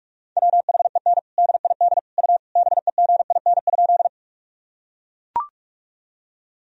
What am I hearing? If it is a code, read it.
Morse code, heard as a keyed tone, about 41 wpm, 700 Hz. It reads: WHEN DID U BEGIN?
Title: Sentences from Top 300 Words and Abbreviations 45wpm